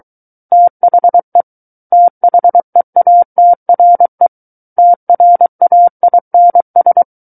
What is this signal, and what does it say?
Morse code, heard as a keyed tone, about 23 wpm, 700 Hz. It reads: THE THEATRE TRAINS
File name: Book One / 14-Chapter 14 - 23 WPM